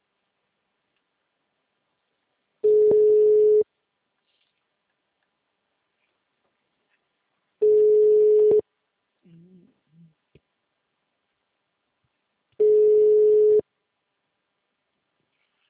Ringback tone (Europe, 425 Hz).